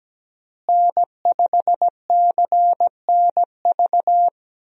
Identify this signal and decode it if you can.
Morse code, heard as a keyed tone, about 17 wpm, 700 Hz. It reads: N5CNV
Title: USA Call Signs - 5 Characters - Repeated 3x - 17wpm